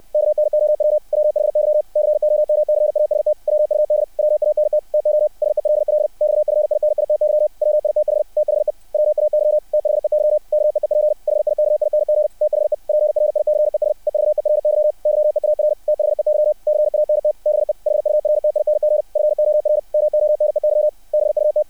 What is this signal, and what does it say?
Morse code, heard as a keyed tone, about 60 wpm, 600 Hz. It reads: KNOW WHY COMCAST WUD WANT TO SELL XFINITY WITH THE WAY THEY KEEP HIKING THE PRICES EVERY YEAR THEY CANT BE HURTING FOR MONEY BUT
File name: archived / CRSnet